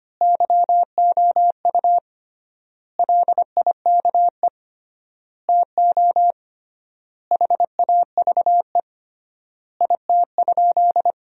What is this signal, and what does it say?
Morse code, heard as a keyed tone, about 25 wpm, 700 Hz. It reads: YOU LIKE TO HAVE IT?
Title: 3x Word-Spacing - Sentences from Top 100 Words - 25wpm